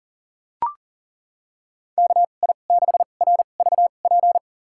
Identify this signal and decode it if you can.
Morse code, heard as a keyed tone, about 40 wpm, spaced as for 29 wpm, 700 Hz. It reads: KI6RVP
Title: USA Call Signs - 6 Characters - 2x Character Spacing - 40wpm